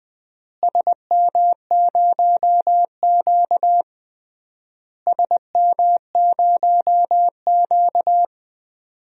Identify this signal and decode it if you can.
Morse code, heard as a keyed tone, about 20 wpm, 690 Hz. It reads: SM0Q SM0Q
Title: DXCC-100 Call Signs - 4 Characters - Repeated 3x - 20wpm